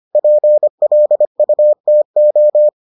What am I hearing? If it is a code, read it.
Morse code, heard as a keyed tone, about 25 wpm, 600 Hz. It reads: PLUTO